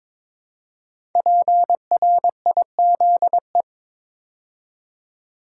Morse code, heard as a keyed tone, about 22 wpm, 700 Hz. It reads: PRIZE
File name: 5 Letter Words - First 3 Letters Given - 22wpm